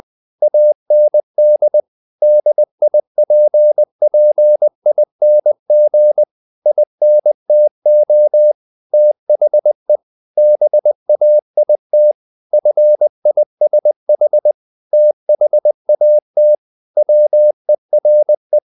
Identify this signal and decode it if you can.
Morse code, heard as a keyed tone, about 20 wpm, 600 Hz. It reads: AND DIPPING INTO THE BAIT FISH THAT WERE
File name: the old man and the sea / the old man and the sea part 1 20wpm